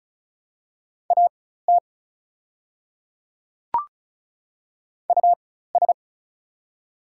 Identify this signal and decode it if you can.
Morse code, heard as a keyed tone, about 36 wpm, spaced as for 16 wpm, 690 Hz. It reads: AT US
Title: 4x Character Spacing - Top 300 Words - 35wpm